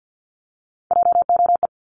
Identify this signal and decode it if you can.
Morse code, heard as a keyed tone, about 50 wpm, 690 Hz. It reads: JOE
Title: Multi-letter Phonemes 50wpm